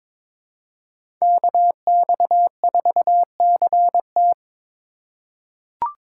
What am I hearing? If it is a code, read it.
Morse code, heard as a keyed tone, about 22 wpm, 700 Hz. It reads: KX4CT
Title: USA Call Signs - 5 Characters - 2x Character Spacing - 22wpm